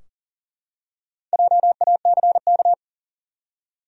Morse code, heard as a keyed tone, about 40 wpm, 700 Hz. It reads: JACK
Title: USA Common Names - Max 5 Letters - 40wpm